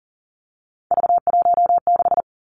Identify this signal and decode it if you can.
Morse code, heard as a keyed tone, about 38 wpm, 700 Hz. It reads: V16
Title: Three Challenging Letter-Number Combinations 40wpm